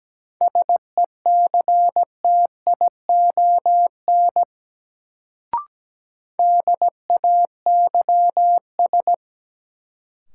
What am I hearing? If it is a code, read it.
Morse code, heard as a keyed tone, about 17 wpm, 700 Hz. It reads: SECTION DAYS
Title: Sentences from Top 1000 Words - Review Part 5 - New Word Drill - 17wpm